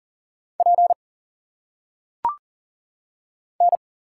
Morse code, heard as a keyed tone, about 38 wpm, 700 Hz. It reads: P N
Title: Single Letters - Phonetic Pronunciation 40wpm